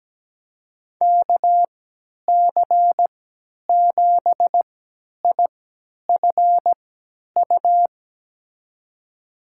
Morse code, heard as a keyed tone, about 17 wpm, spaced as for 10 wpm, 700 Hz. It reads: KC7IFU
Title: USA Call Signs - 6 Characters - 3x Character Spacing - 17wpm